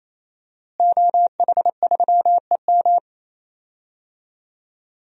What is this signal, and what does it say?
Morse code, heard as a keyed tone, about 28 wpm, 700 Hz. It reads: OH3EM
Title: DXCC-100 Call Signs - 5 Characters - Repeated 3x - 28wpm